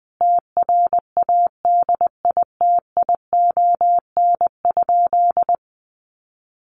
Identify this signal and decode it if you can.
Morse code, heard as a keyed tone, about 20 wpm, 700 Hz. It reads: TRADITION?
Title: Sentences from Top 1500 Words 20wpm